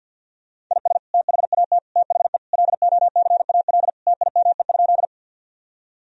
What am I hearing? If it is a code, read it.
Morse code, heard as a keyed tone, about 50 wpm, 700 Hz. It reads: IS THAT THE LOCAL TIME?
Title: Sentences from Top 1000 Words - Review Part 3 - 50wpm